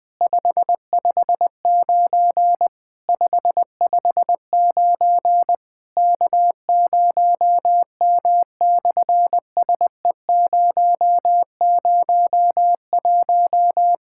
Morse code, heard as a keyed tone, about 19 wpm, 700 Hz. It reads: 559 559 K0M/SE001
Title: SOTA QSO Practice 20wpm